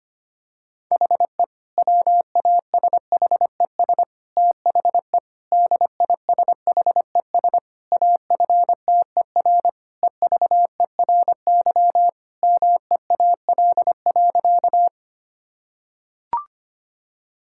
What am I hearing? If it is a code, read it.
Morse code, heard as a keyed tone, about 25 wpm, 700 Hz. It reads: HE WASHES THE DISHES AFTER EVERY MEAL.
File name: practice / 25wpm / Difficult Sentences with Hints - 25wpm